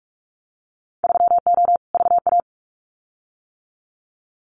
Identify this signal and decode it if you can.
Morse code, heard as a keyed tone, about 44 wpm, 700 Hz. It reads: 3O VA
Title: Winter Field Day Exchanges 45wpm